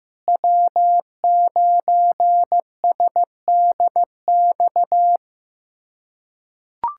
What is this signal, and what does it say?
Morse code, heard as a keyed tone, about 15 wpm, 700 Hz. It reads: W9SDX
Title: USA Callsign DE Callsign - 5 Characters - 15wpm